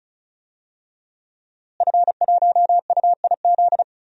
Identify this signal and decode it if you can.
Morse code, heard as a keyed tone, about 36 wpm, 700 Hz. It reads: F1UIZ
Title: DXCC-100 Call Signs - 5 Characters - Repeated 2x - 35wpm